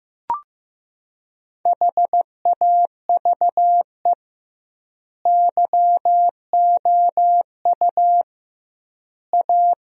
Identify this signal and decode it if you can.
Morse code, heard as a keyed tone, about 15 wpm, 700 Hz. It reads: HAVE YOU A
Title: 2x Word-Spacing - Sentences from Top 400 Words - 15wpm